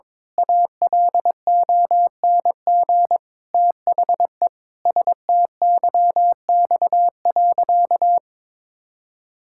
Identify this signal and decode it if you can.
Morse code, heard as a keyed tone, about 22 wpm, 700 Hz. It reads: ALONG THE STYX.
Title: Pangram Challenge v2 - 22wpm